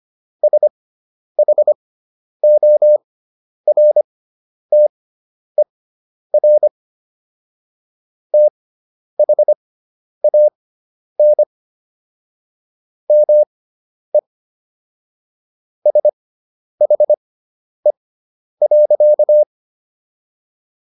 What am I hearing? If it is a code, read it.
Morse code, heard as a keyed tone, about 25 wpm, spaced as for 10 wpm, 600 Hz. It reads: SHORTER THAN ME SHE.